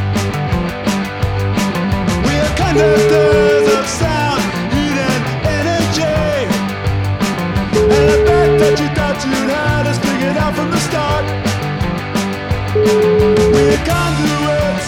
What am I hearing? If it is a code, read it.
Ringback tone (Europe, 425 Hz), with noise.